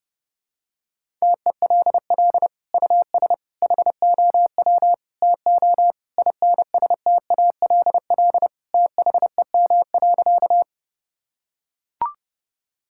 Morse code, heard as a keyed tone, about 30 wpm, 700 Hz. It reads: TELL US HOW TO INSTALL THEM.
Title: Sentences from Top 1600 Words 30wpm